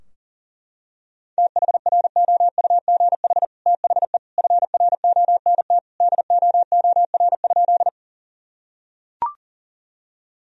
Morse code, heard as a keyed tone, about 40 wpm, 700 Hz. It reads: THROUGH THE FRONT DOOR?
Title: Sentences from Top 1000 Words - Review Part 2 - 40wpm